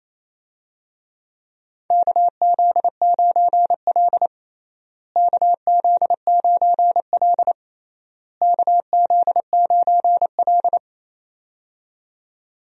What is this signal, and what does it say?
Morse code, heard as a keyed tone, about 28 wpm, 700 Hz. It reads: KZ9L KZ9L KZ9L
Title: USA Call Signs - 4 Characters - Repeated 3x - 28wpm